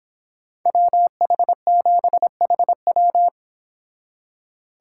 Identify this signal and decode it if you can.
Morse code, heard as a keyed tone, about 30 wpm, 700 Hz. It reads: WH7HW